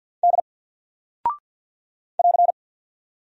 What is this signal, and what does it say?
Morse code, heard as a keyed tone, about 48 wpm, 700 Hz. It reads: D <AR>
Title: Single Character (Letter:Number:Punctuation:Prosign) Alt 50wpm